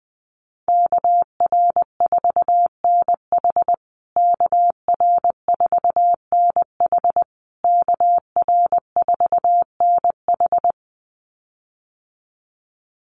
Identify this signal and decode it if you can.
Morse code, heard as a keyed tone, about 20 wpm, 700 Hz. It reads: KR4NH KR4NH KR4NH
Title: USA Call Signs Repeated 3x 20wpm